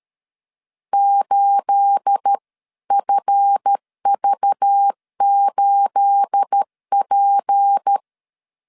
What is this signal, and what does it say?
Morse code, heard as a keyed tone, about 13 wpm, 800 Hz. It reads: 8FV8P
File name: CW / DET / 1 9 1500C S B 12wpm 3819